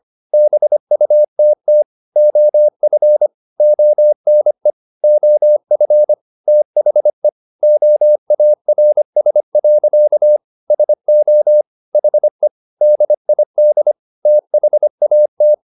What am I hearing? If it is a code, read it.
Morse code, heard as a keyed tone, about 25 wpm, 600 Hz. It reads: BUTT OF ONE OF THE OARS. SO HE DID THAT